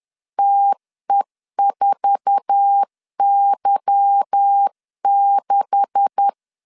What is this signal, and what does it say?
Morse code, heard as a keyed tone, about 10 wpm, 800 Hz. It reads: TE4Y6
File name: LEZIONI / 9 286 10wpm 2735